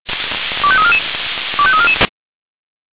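Five-tone selective call (ZVEI), heard as quick runs of five tones, tones of about 70 ms. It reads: 25344 25344
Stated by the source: Received with an SDR; ZVEI five-tone selective call.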